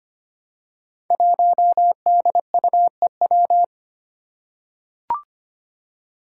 Morse code, heard as a keyed tone, about 25 wpm, 700 Hz. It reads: 1DUEW
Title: Five Challenging Letter-Number Combinations 25wpm